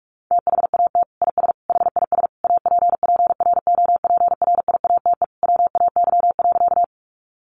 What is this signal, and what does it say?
Morse code, heard as a keyed tone, about 44 wpm, 700 Hz. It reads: THAT IS HIS APPROPRIATE WAY.
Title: Sentences from Top 900 Words 45wpm